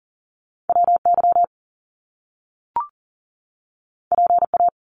Morse code, heard as a keyed tone, about 38 wpm, 700 Hz. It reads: WY PA
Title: US States 40wpm